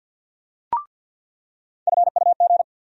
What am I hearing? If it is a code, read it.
Morse code, heard as a keyed tone, about 48 wpm, 700 Hz. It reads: FUG
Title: USA Call Sign Postfixes - 3 Letters 50wpm